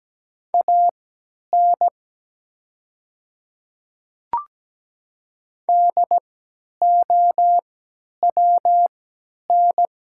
Morse code, heard as a keyed tone, about 17 wpm, spaced as for 10 wpm, 700 Hz. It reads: AN DOWN
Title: 3x Character Spacing - Top 100 Words - 17wpm